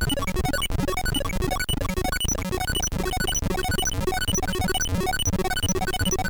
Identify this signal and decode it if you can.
There is no code.